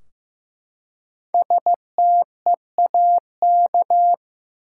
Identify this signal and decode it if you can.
Morse code, heard as a keyed tone, about 15 wpm, 700 Hz. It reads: STEAK
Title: Five Letter Words - 15wpm